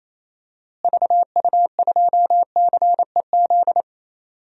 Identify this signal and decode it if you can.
Morse code, heard as a keyed tone, about 28 wpm, 700 Hz. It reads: VU2CEZ
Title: DXCC-100 Call Signs - 6 Characters - Repeated 2x - 28wpm